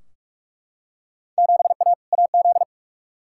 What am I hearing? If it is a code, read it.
Morse code, heard as a keyed tone, about 41 wpm, 700 Hz. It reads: 7A AZ
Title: Field Day Exchanges - 45wpm